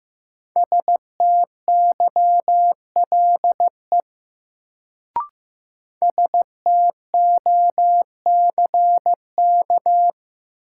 Morse code, heard as a keyed tone, about 15 wpm, 700 Hz. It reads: STYLE STOCK
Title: Sentences from Top 800 Words - New Word Drill - 15wpm